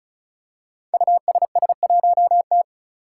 Morse code, heard as a keyed tone, about 36 wpm, 700 Hz. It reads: USS1T
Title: Five Challenging Letter-Number Combinations 35wpm